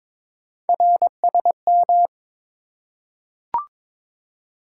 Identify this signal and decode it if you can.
Morse code, heard as a keyed tone, about 22 wpm, 700 Hz. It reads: RSM
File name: practice / 22wpm / Three Letter-Number Combinations - 22wpm